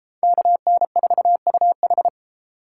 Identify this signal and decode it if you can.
Morse code, heard as a keyed tone, about 30 wpm, 700 Hz. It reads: KN4UH